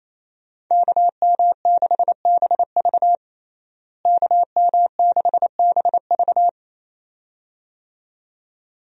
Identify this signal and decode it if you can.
Morse code, heard as a keyed tone, about 28 wpm, 700 Hz. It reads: KM6BV KM6BV
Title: USA Call Signs - 5 Characters - Repeated 2x - 28wpm